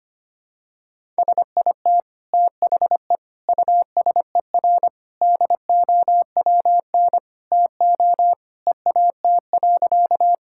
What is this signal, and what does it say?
Morse code, heard as a keyed tone, about 25 wpm, 700 Hz. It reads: SIT THE USER DOWN TO EAT.
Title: Sentences from Top 1000 Words - Review Part 5 - 25wpm